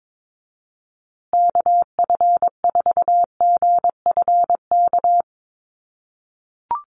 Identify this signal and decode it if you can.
Morse code, heard as a keyed tone, about 22 wpm, 700 Hz. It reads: KF4GFK
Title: USA Call Signs Repeated 2x 22wpm